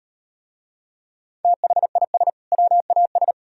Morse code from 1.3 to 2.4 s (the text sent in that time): THIS